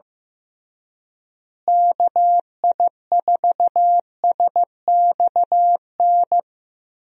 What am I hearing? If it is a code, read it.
Morse code, heard as a keyed tone, about 15 wpm, 700 Hz. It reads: KI4SXN
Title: USA Call Signs - 6 Characters - Repeated 3x - 15wpm